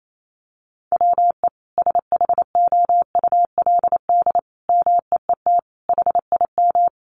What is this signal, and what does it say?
Morse code, heard as a keyed tone, about 28 wpm, 700 Hz. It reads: WE SHOULD MEET HIM
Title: Sentences from Top 1100 Words 28wpm